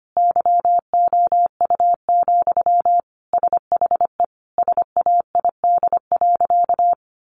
Morse code, heard as a keyed tone, about 25 wpm, 700 Hz. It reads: YOU, SHE SAID.
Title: Sentences from Top 600 Words 25wpm